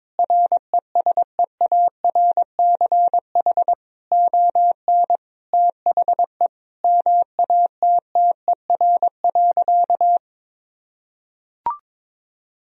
Morse code, heard as a keyed tone, about 21 wpm, 700 Hz. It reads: RESEARCH ON THE MATTER.
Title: Sentences from Top 1000 Words - Review Part 4 - 22wpm